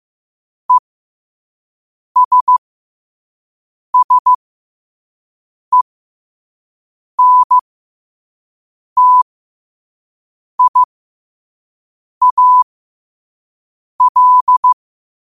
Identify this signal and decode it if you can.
Morse code, heard as a keyed tone, about 15 wpm, spaced as for 5 wpm, 1000 Hz. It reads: ESSENTIAL
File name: qotdfiles / 5wpm